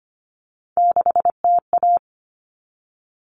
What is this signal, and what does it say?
Morse code, heard as a keyed tone, about 25 wpm, 700 Hz. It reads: 6TA
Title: All Cut Numbers - 3-Digit 25wpm